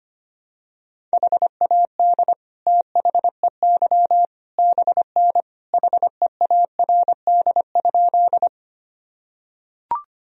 Morse code, heard as a keyed tone, about 25 wpm, 700 Hz. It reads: HAD THEY BN HEARD?
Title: Sentences from Top 400 Words and Abbreviations 25wpm